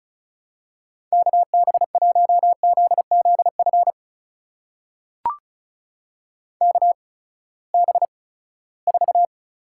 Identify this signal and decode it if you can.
Morse code, heard as a keyed tone, about 36 wpm, spaced as for 20 wpm, 700 Hz. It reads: KB1ZZF KB4
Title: USA Call Signs - 6 Characters - 8x Character Spacing - 35wpm